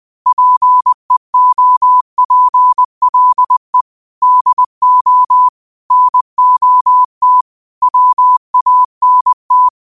Morse code, heard as a keyed tone, about 20 wpm, 1000 Hz. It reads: PEOPLE DO NOT WANT
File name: qotdfiles / 20wpm